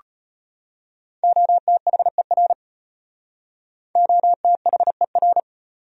Morse code, heard as a keyed tone, about 32 wpm, 700 Hz. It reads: OTHER OTHER